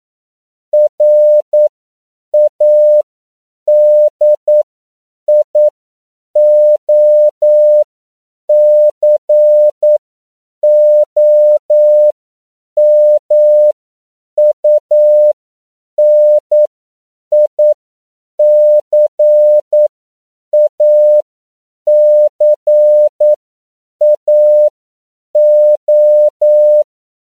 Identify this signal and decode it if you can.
Morse code, heard as a keyed tone, about 9 wpm, spaced as for 7 wpm, 600 Hz. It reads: RADIOCOMUNICACAO
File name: CW / 10PPM